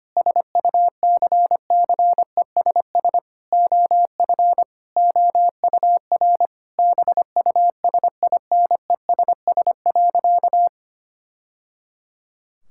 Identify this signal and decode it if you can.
Morse code, heard as a keyed tone, about 25 wpm, 700 Hz. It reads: SUCCESS OF OUR BUSINESS.
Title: Sentences from Top 1000 Words - Review Part 6 - 25wpm